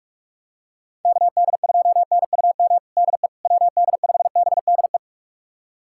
Morse code, heard as a keyed tone, about 41 wpm, 700 Hz. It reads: KD2NUM DE WD5BDE
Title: USA Callsign DE Callsign 45wpm